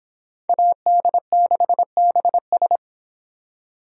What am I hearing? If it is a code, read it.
Morse code, heard as a keyed tone, about 29 wpm, 700 Hz. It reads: AD6BS